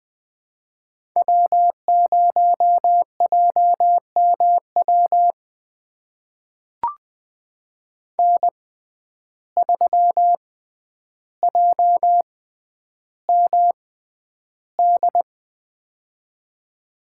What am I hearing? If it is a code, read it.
Morse code, heard as a keyed tone, about 20 wpm, spaced as for 10 wpm, 700 Hz. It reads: W0JMW N3JMD
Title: USA Call Signs - 5 Characters - 6x Character Spacing - 20wpm